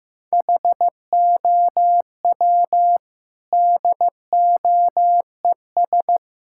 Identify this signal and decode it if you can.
Morse code, heard as a keyed tone, about 15 wpm, 700 Hz. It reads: HOW DOES
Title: Sentences from Top 1600 Words 15wpm